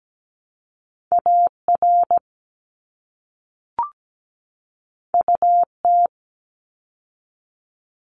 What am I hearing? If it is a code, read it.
Morse code, heard as a keyed tone, about 17 wpm, 700 Hz. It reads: AR UT
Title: ARRL Sections 17wpm